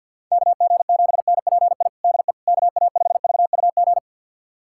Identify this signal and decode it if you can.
Morse code, heard as a keyed tone, about 53 wpm, 700 Hz. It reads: KG7NPI DE KA5VUG
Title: USA Callsign DE Callsign - 6 Characters - 50wpm